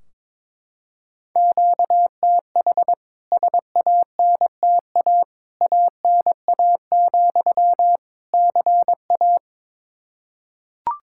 Morse code, heard as a keyed tone, about 22 wpm, 700 Hz. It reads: QTH SANTA ANA, CA
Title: K1USN SST QSO Practice 22wpm